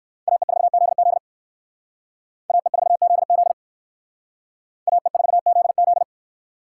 Morse code, heard as a keyed tone, about 57 wpm, 700 Hz. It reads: AE4ZZ AE4ZZ AE4ZZ